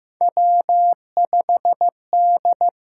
Morse code, heard as a keyed tone, about 15 wpm, 700 Hz. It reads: W5D
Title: USA Call Signs - 4 Characters - 6x Character Spacing - 15wpm